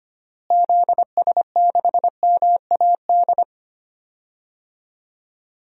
Morse code, heard as a keyed tone, about 25 wpm, 700 Hz. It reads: ZS6MAD
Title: DXCC-100 Call Signs - 6 Characters - Repeated 2x - 25wpm